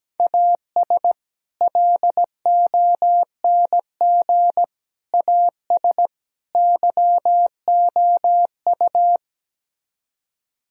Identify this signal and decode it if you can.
Morse code, heard as a keyed tone, about 17 wpm, 700 Hz. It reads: AS LONG AS YOU
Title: Speed-Racing - Sets of 4 Words 17wpm